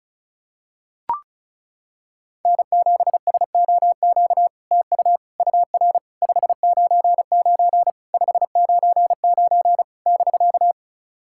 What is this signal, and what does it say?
Morse code, heard as a keyed tone, about 36 wpm, 700 Hz. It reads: N7SOQ TU UR 599 599 <BK>
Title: SOTA QSO Practice 35wpm